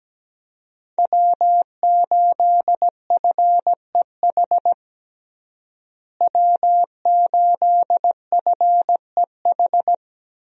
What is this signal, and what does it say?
Morse code, heard as a keyed tone, about 17 wpm, 700 Hz. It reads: W8FEH W8FEH
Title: USA Call Signs - 5 Characters - Repeated 2x - 17wpm